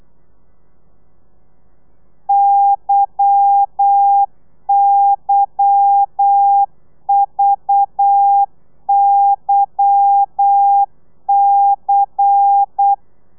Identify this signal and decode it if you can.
Morse code, heard as a keyed tone, about 8 wpm, 800 Hz. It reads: YYVYC
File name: telegrafia / i2tfj / Lez10 35